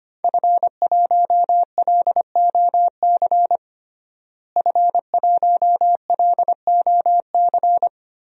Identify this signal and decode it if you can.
Morse code, heard as a keyed tone, about 25 wpm, 700 Hz. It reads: F1LOC F1LOC
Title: DXCC-100 Call Signs - 5 Characters - Repeated 3x - 25wpm